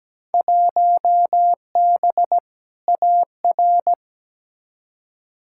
Morse code, heard as a keyed tone, about 17 wpm, 700 Hz. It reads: 1B AR
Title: Field Day Exchanges Encore - 17wpm